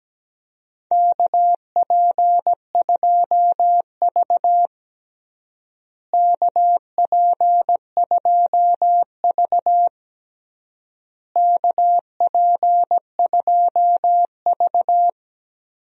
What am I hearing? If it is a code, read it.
Morse code, heard as a keyed tone, about 17 wpm, 700 Hz. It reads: KP2V KP2V KP2V
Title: DXCC-100 Call Signs - 4 Characters - Repeated 3x - 17wpm